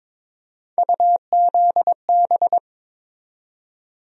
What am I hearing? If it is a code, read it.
Morse code, heard as a keyed tone, about 22 wpm, 700 Hz. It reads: UZB
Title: USA Call Sign Postfixes - 3 Letters 22wpm